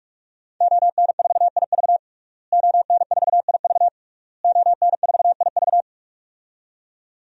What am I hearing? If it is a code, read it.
Morse code, heard as a keyed tone, about 46 wpm, 700 Hz. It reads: ON4IV ON4IV ON4IV
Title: DXCC-100 Call Signs - 5 Characters - Repeated 3x - 45wpm